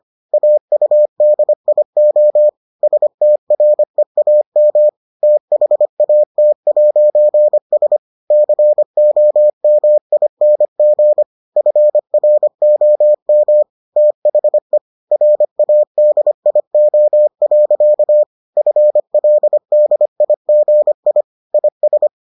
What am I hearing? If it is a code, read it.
Morse code, heard as a keyed tone, about 25 wpm, 600 Hz. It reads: AUDIO STREAM THAT'S COMING FROM THE RADIO. FLDIGI IS